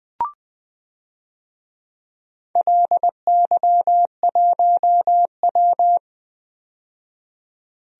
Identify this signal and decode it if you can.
Morse code, heard as a keyed tone, about 20 wpm, 700 Hz. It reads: LY1W
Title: DXCC-100 Call Signs - 4 Characters - 20wpm